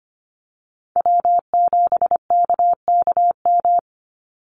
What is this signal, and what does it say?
Morse code, heard as a keyed tone, about 25 wpm, 700 Hz. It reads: W7KKM
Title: NA SOTA Chaser Callsigns 25wpm